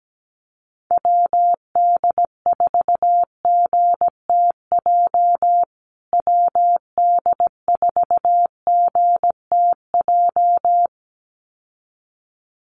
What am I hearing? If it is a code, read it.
Morse code, heard as a keyed tone, about 17 wpm, 700 Hz. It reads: WD4GTJ WD4GTJ
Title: USA Call Signs Repeated 2x 17wpm